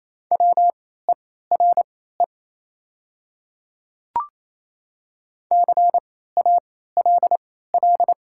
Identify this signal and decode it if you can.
Morse code, heard as a keyed tone, about 28 wpm, spaced as for 16 wpm, 700 Hz. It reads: WERE CALL
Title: 3x Character Spacing - Top 100 Words - 28wpm